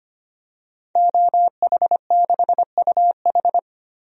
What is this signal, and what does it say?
Morse code, heard as a keyed tone, about 25 wpm, 700 Hz. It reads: OH6UH
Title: DXCC-100 Call Signs - 5 Characters - Repeated 2x - 25wpm